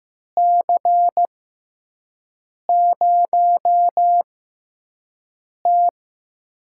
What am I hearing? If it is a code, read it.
Morse code, heard as a keyed tone, about 15 wpm, spaced as for 5 wpm, 700 Hz. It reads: C0T
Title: USA Call Signs - 4 Characters - 6x Character Spacing - 15wpm